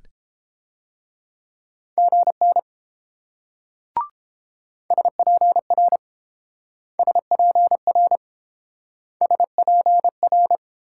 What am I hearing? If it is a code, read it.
Morse code, heard as a keyed tone, about 29 wpm, 700 Hz. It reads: GN SPR SPR SPR